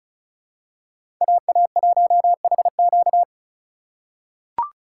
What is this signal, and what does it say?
Morse code, heard as a keyed tone, about 36 wpm, 700 Hz. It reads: AA1HQ
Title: USA Call Signs - 5 Characters - 6x Character Spacing - 35wpm